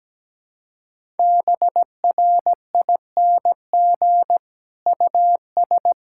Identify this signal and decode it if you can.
Morse code, heard as a keyed tone, about 17 wpm, 700 Hz. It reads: BRING US
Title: Sentences from Top 1000 Words - Review Part 6 - 17wpm